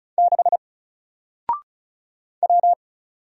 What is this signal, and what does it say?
Morse code, heard as a keyed tone, about 36 wpm, 700 Hz. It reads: 6 W
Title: Single Letter-Number 35wpm